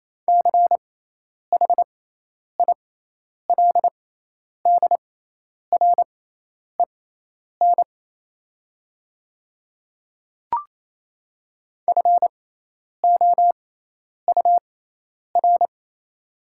Morse code, heard as a keyed tone, about 28 wpm, spaced as for 10 wpm, 700 Hz. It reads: CHILDREN FOUR
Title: 6x Character Spacing - Top 500 Words - 28wpm